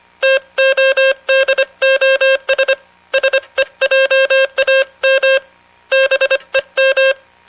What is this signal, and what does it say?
Morse code, heard as a keyed tone, about 25 wpm, 540 Hz. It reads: TODOS SEJAM BEM